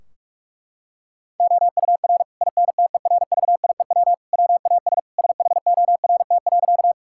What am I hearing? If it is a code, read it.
Morse code, heard as a keyed tone, about 46 wpm, 700 Hz. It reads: OUR INTERVIEW WAS SHORT.
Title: Sentences from Top 1600 Words 45wpm